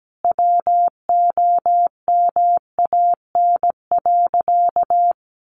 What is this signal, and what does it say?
Morse code, heard as a keyed tone, about 17 wpm, 700 Hz. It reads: WOMAN.
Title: Sentences from Top 1100 Words 17wpm